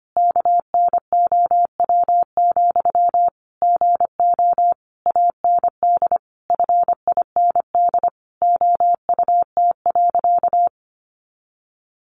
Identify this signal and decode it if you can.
Morse code, heard as a keyed tone, about 25 wpm, 700 Hz. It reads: KNOW, GO AND FIND OUT.
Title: Sentences from Top 200 Words 25wpm